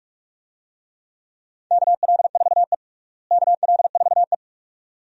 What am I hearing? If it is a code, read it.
Morse code, heard as a keyed tone, about 46 wpm, 700 Hz. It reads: KL4E KL4E
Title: USA Call Signs - 4 Characters - Repeated 2x - 45wpm